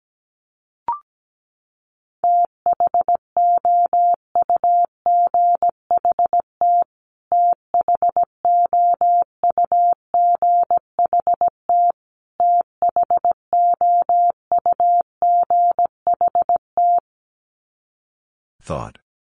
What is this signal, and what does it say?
Morse code, heard as a keyed tone, about 17 wpm, 700 Hz. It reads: THOUGHT THOUGHT THOUGHT
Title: Top 500 Words Repeated 17wpm